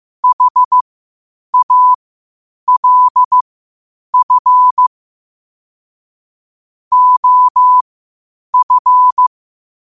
Morse code, heard as a keyed tone, about 15 wpm, spaced as for 8 wpm, 1000 Hz. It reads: HALF OF
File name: qotdfiles / 8wpm